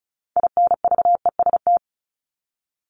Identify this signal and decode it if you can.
Morse code, heard as a keyed tone, about 34 wpm, 700 Hz. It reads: INVEST
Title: Top 1001-1250 Words 35wpm